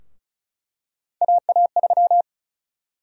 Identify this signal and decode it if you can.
Morse code, heard as a keyed tone, about 36 wpm, 700 Hz. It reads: AA3
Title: USA Call Sign Prefixes - 35wpm